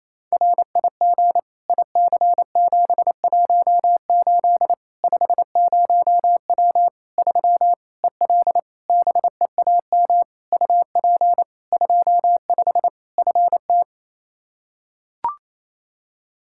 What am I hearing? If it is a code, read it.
Morse code, heard as a keyed tone, about 28 wpm, 700 Hz. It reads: RIG IC718 50W 3 EL BEAM UP 25 FT
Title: Practice QSO - Exchange 2 - Info - 28wpm